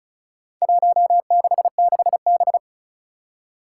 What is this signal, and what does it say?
Morse code, heard as a keyed tone, about 36 wpm, 700 Hz. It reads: 166B
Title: Four Challenging Letter-Number Combinations 35wpm